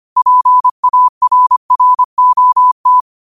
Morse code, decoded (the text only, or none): PARROT